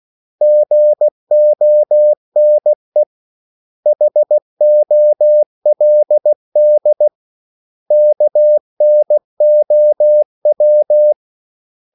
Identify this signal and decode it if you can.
Morse code, heard as a keyed tone, about 16 wpm, 600 Hz. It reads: GONE HOLD KNOW
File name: CW / Practice / 500MostCommonEnglishWords16